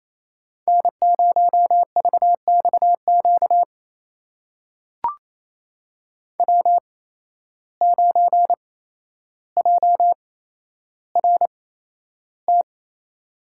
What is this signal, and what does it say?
Morse code, heard as a keyed tone, about 28 wpm, spaced as for 12 wpm, 700 Hz. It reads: N0VXQ W9JRT
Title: USA Call Signs - 5 Characters - 8x Character Spacing - 28wpm